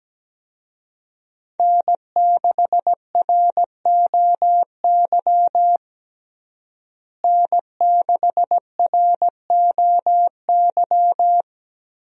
Morse code, heard as a keyed tone, about 17 wpm, 700 Hz. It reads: N6ROY N6ROY
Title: USA Call Signs - 5 Characters - Repeated 3x - 17wpm